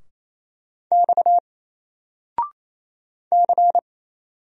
Morse code, heard as a keyed tone, about 27 wpm, 700 Hz. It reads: X C
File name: Single-Letter-Number-28wpm